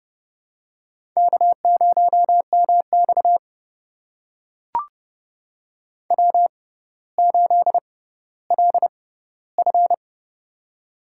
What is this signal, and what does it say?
Morse code, heard as a keyed tone, about 30 wpm, spaced as for 15 wpm, 700 Hz. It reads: K0MX W8LF
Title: USA Call Signs - 4 Characters - 6x Character Spacing - 30wpm